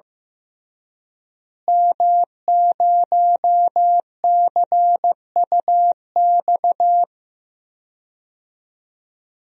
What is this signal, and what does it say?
Morse code, heard as a keyed tone, about 15 wpm, 700 Hz. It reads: M0CUX
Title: DXCC-100 Call Signs - 5 Characters - Repeated 3x - 15wpm